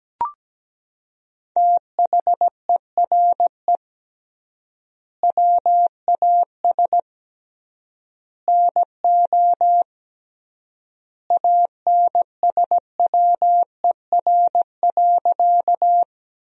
Morse code, heard as a keyed tone, about 17 wpm, 700 Hz. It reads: THERE WAS NO ANSWER.
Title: 3x Word-Spacing - Sentences from Top 300 Words - 17wpm